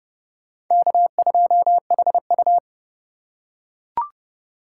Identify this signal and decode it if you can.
Morse code, heard as a keyed tone, about 29 wpm, 700 Hz. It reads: K2HU